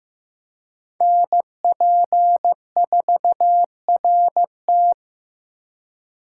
Morse code, heard as a keyed tone, about 18 wpm, 700 Hz. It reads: NP4RT
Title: Speed Racing - USA Call Signs 15wpm